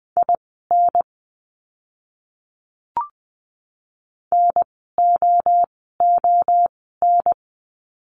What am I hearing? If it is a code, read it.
Morse code, heard as a keyed tone, about 20 wpm, spaced as for 14 wpm, 700 Hz. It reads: IN NOON
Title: 030 Lesson 6f - Letter i - Words 20wpm